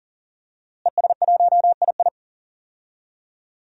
Morse code, heard as a keyed tone, about 40 wpm, 700 Hz. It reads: ES1II
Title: DXCC-100 Call Signs - 5 Characters - Repeated 3x - 40wpm